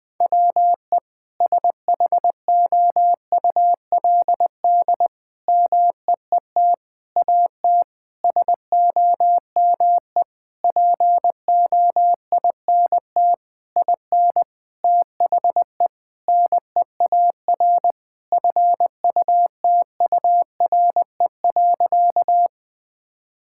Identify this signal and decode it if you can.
Morse code, heard as a keyed tone, about 20 wpm, 700 Hz. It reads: WE SHOULD MEET AT SOME POINT IN THE NEAR FUTURE.
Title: Sentences with 7-Word Phrases - 20wpm